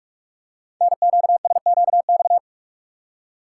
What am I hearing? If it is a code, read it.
Morse code, heard as a keyed tone, about 44 wpm, 700 Hz. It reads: NQSQX
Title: Code Groups - 5 Letters - 45wpm